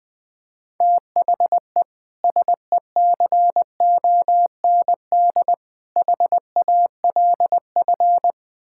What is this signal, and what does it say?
Morse code, heard as a keyed tone, about 20 wpm, 700 Hz. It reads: THE SECOND HALF
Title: Build A Phrase - 5-Word Phrases - 20wpm